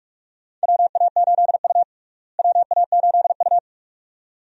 Morse code, heard as a keyed tone, about 42 wpm, 700 Hz. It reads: WA8U WA8U
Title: USA Call Signs - 4 Characters - Repeated 2x - 45wpm